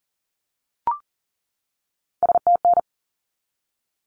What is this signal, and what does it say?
Morse code, heard as a keyed tone, about 40 wpm, 700 Hz. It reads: STN
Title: Abbreviations 40wpm